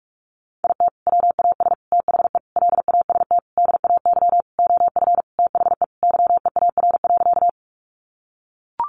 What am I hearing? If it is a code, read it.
Morse code, heard as a keyed tone, about 44 wpm, 700 Hz. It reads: IT WAS THE LAST DAY OF THE YEAR.
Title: Sentences from Top 300 Words 45wpm